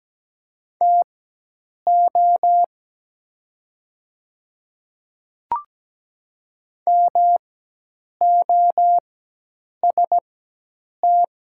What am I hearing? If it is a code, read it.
Morse code, heard as a keyed tone, about 17 wpm, spaced as for 8 wpm, 690 Hz. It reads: TO MOST
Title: 4x Character Spacing - Top 100 Words - 17wpm